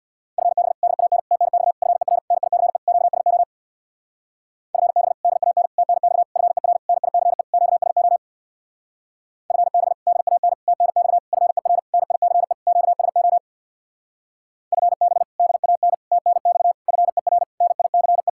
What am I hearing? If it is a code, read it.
Morse code, heard as a keyed tone, about 64 wpm, 700 Hz. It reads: FB DAN TNX FER NICE QSO FB DAN TNX FER NICE QSO FB DAN TNX FER NICE QSO FB DAN TNX FER NICE